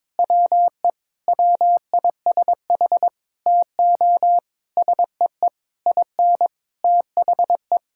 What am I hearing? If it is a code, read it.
Morse code, heard as a keyed tone, about 22 wpm, 700 Hz. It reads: WE WISH TO SEE IN THE
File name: Sets of 9 Words - 22wpm